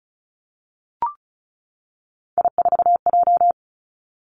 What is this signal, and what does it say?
Morse code, heard as a keyed tone, about 36 wpm, 700 Hz. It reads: I4J
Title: Three Challenging Letter-Number Combinations 35wpm